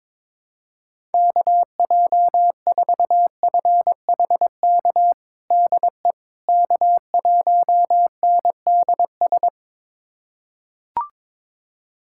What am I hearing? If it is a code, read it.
Morse code, heard as a keyed tone, about 22 wpm, 700 Hz. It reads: KJ4FHK DE K1NDS
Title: USA Callsign DE Callsign 22wpm